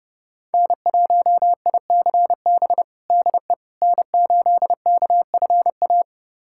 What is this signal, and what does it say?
Morse code, heard as a keyed tone, about 30 wpm, 700 Hz. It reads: N1ICB DE N8KFA
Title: USA Callsign DE Callsign - 5 Characters - 30wpm